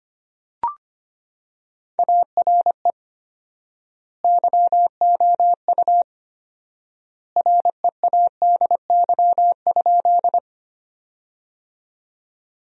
Morse code, heard as a keyed tone, about 25 wpm, 700 Hz. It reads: ARE YOU READY?
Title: 4x Word-Spacing - Sentences from Top 400 Words - 25wpm